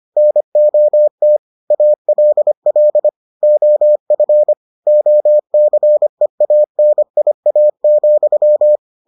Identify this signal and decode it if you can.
Morse code, heard as a keyed tone, about 25 wpm, 600 Hz. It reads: NOT ALL OF OCEANIA,